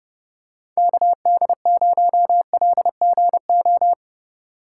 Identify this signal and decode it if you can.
Morse code, heard as a keyed tone, about 30 wpm, 690 Hz. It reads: KD0LGO
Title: USA Call Signs - 4x Character Spacing - 30wpm